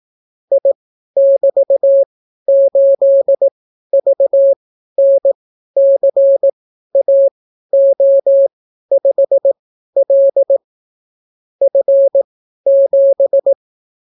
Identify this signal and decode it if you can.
Morse code, heard as a keyed tone, about 18 wpm, spaced as for 12 wpm, 550 Hz. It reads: I<BT>8VNCAO5L F7